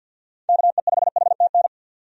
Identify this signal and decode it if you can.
Morse code, heard as a keyed tone, about 48 wpm, 700 Hz. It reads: KE5HTN
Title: USA Call Signs - 6 Characters - 50wpm